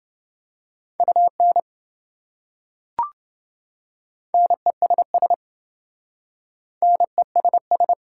Morse code, heard as a keyed tone, about 27 wpm, 700 Hz. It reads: UN NESS NESS